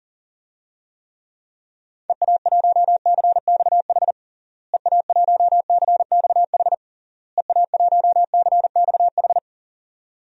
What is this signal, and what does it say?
Morse code, heard as a keyed tone, about 40 wpm, 700 Hz. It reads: EA1CXH EA1CXH EA1CXH
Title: DXCC-100 Call Signs - 6 Characters - Repeated 3x - 40wpm